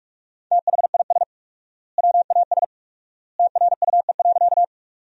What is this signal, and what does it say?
Morse code, heard as a keyed tone, about 45 wpm, 700 Hz. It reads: THIS WAS TRUE.
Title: 4x Word-Spacing - Sentences from Top 400 Words - 45wpm